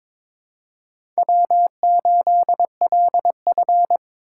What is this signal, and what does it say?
Morse code, heard as a keyed tone, about 22 wpm, 700 Hz. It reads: W8LF
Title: USA Call Signs - 2x Character Spacing - 22wpm